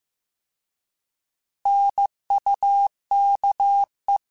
Morse code, heard as a keyed tone, about 15 wpm, 800 Hz. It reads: NUKE
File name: Alphabetical / Alphabetical mp3s / Ns @13